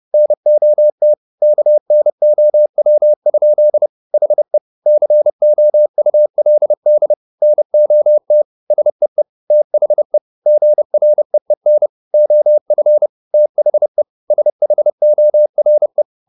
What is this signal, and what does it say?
Morse code, heard as a keyed tone, about 30 wpm, 600 Hz. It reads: NOT KNOW? HE COULD NOT SEE THE GREEN OF THE SHORE
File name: the old man and the sea / the old man and the sea 1 30wpm